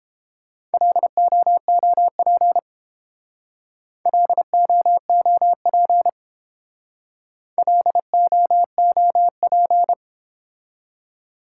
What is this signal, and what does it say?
Morse code, heard as a keyed tone, about 30 wpm, 700 Hz. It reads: LOOP LOOP LOOP